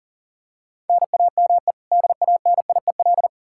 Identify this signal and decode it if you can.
Morse code, heard as a keyed tone, about 40 wpm, 700 Hz. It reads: NAME DANIEL
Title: Practice QSO - Exchange 1 - Beginning - 40wpm